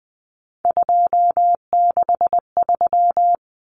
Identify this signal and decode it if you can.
Morse code, heard as a keyed tone, about 19 wpm, 700 Hz. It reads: 263
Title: Numbers - 3-Digit 20wpm